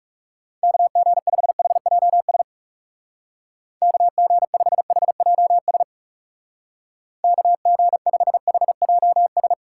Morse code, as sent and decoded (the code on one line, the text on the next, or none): -.- --. ..... .... .--- ... / -.- --. ..... .... .--- ... / -.- --. ..... .... .--- ...
KG5HJS KG5HJS KG5HJS